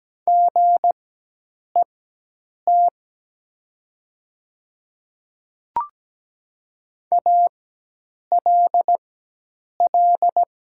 Morse code, heard as a keyed tone, about 17 wpm, spaced as for 8 wpm, 700 Hz. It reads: GET ALL
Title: 4x Character Spacing - Top 200 Words - 17wpm